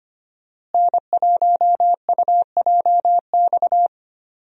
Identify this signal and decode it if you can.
Morse code, heard as a keyed tone, about 25 wpm, 700 Hz. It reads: N1UJX
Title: USA Call Signs - 5 Characters - Repeated 2x - 25wpm